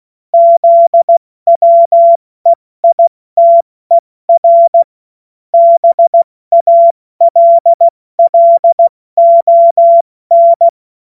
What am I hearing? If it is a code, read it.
Morse code, heard as a keyed tone, about 16 wpm, 680 Hz. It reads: ZWEITER BALLON
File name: Morsekurs / Witze / Witz 03. Mai 2022 70 F80